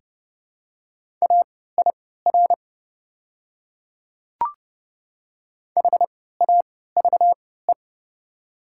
Morse code, heard as a keyed tone, about 30 wpm, spaced as for 17 wpm, 700 Hz. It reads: AIR HAVE
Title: 3x Character Spacing - Top 500 Words - 30wpm